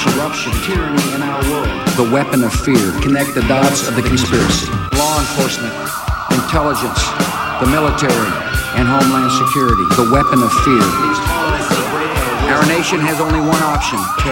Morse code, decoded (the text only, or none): none